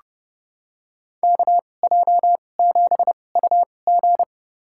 Morse code, heard as a keyed tone, about 32 wpm, spaced as for 22 wpm, 700 Hz. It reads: KJ7UG
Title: USA Call Signs - 2x Character Spacing - 30wpm